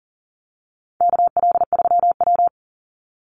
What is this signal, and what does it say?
Morse code, heard as a keyed tone, about 40 wpm, 700 Hz. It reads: KL3W
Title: Call Signs Repeated 3x - 40wpm